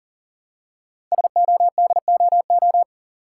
Morse code, heard as a keyed tone, about 39 wpm, 690 Hz. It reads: SODOO
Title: Code Groups - 5 Letters - 40wpm